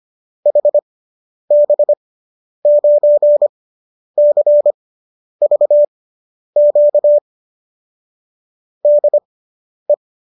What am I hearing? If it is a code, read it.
Morse code, heard as a keyed tone, about 25 wpm, spaced as for 10 wpm, 590 Hz. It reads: HB9CVQ DE